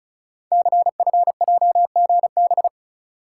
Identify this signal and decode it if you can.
Morse code, heard as a keyed tone, about 36 wpm, 690 Hz. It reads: CFJGB
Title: Code Groups - 5 Letters - 35wpm